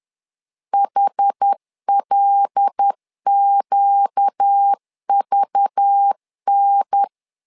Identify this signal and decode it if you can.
Morse code, heard as a keyed tone, about 10 wpm, 800 Hz. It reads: HLQVN
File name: LEZIONI / 5 LBVC / 5 LBV 10wpm 2248